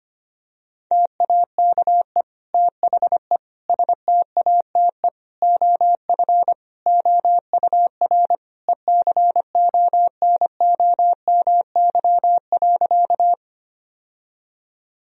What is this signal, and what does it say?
Morse code, heard as a keyed tone, about 25 wpm, 700 Hz. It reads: TAKE THE STATE OF OUR ECONOMY.
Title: Sentences from Top 1000 Words - Review Part 3 - 25wpm